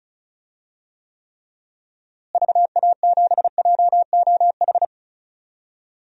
Morse code, heard as a keyed tone, about 34 wpm, 700 Hz. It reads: VA7JOH
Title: DXCC-100 Call Signs - 6 Characters - Repeated 2x - 35wpm